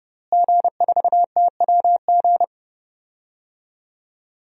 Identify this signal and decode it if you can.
Morse code, heard as a keyed tone, about 30 wpm, 700 Hz. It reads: G4TWG
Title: DXCC-100 Call Signs - 5 Characters - Repeated 2x - 30wpm